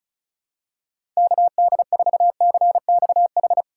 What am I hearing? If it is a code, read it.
Morse code, heard as a keyed tone, about 34 wpm, 690 Hz. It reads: KD4CXH
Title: USA Call Signs - 6 Characters - 4x Character Spacing - 35wpm